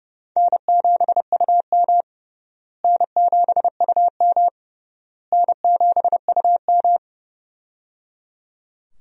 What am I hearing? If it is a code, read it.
Morse code, heard as a keyed tone, about 30 wpm, 700 Hz. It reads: N7UM N7UM N7UM
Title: USA Call Signs - 4 Characters - Repeated 3x - 30wpm